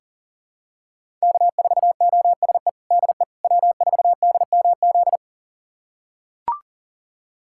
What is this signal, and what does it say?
Morse code, heard as a keyed tone, about 40 wpm, 700 Hz. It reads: K4OSE DE W4DMZ
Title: USA Callsign DE Callsign - 5 Characters - 40wpm